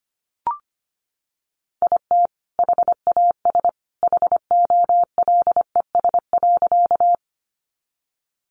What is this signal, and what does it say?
Morse code, heard as a keyed tone, about 25 wpm, 700 Hz. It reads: IT HAS HOLES.
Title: Additional Easy Phrases 25wpm